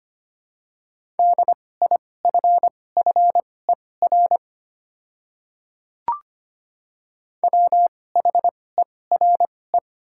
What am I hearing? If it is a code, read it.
Morse code, heard as a keyed tone, about 25 wpm, spaced as for 18 wpm, 700 Hz. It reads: DIFFER WHERE
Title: 2x Character Spacing - Top 300 Words - 25wpm